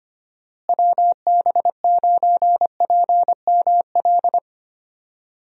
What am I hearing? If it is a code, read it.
Morse code, heard as a keyed tone, about 25 wpm, 700 Hz. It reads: WB9PML
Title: USA Call Signs - 6 Characters - Repeated 2x - 25wpm